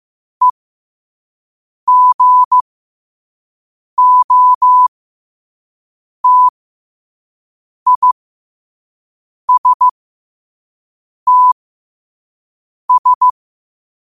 Morse code, heard as a keyed tone, about 15 wpm, spaced as for 5 wpm, 1000 Hz. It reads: EGOTISTS
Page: Quotes for Wed, 13 Aug 2025 in Morse Code at 5 words per minute.